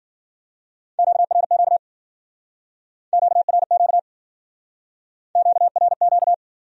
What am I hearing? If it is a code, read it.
Morse code, heard as a keyed tone, about 52 wpm, 700 Hz. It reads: QRQ QRQ QRQ